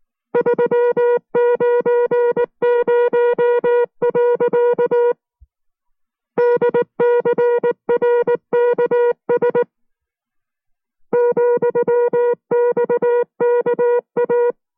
Morse code, heard as a keyed tone, about 19 wpm, 480 Hz. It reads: 390. DCRKS ,XKA